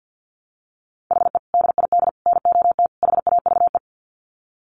Morse code, heard as a keyed tone, about 49 wpm, 700 Hz. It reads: HE DID NOT HAVE
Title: Sets of 4 Words 50wpm